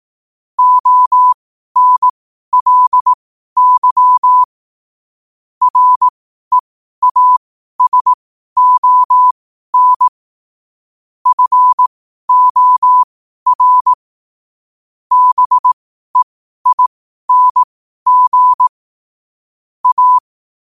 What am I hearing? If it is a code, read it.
Morse code, heard as a keyed tone, about 18 wpm, spaced as for 12 wpm, 1000 Hz. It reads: ONLY REASON FOR BEING A
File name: qotdfiles / 12wpm